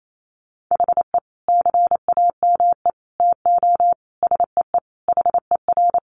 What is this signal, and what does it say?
Morse code, heard as a keyed tone, about 28 wpm, 700 Hz. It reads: HE CAME TO SEE HER
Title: Sentences from Top 1300 Words 28wpm